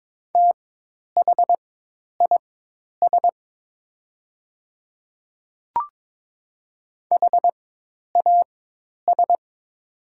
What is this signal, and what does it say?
Morse code, heard as a keyed tone, about 22 wpm, spaced as for 10 wpm, 700 Hz. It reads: THIS HAS
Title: 4x Character Spacing - Top 100 Words - 22wpm